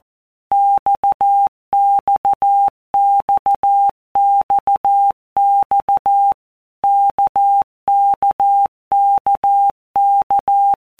Morse code, heard as a keyed tone, about 14 wpm, 800 Hz. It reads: XXXXX KKKK
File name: All Groups 1-minute Blocks / Groups@13 / RKPX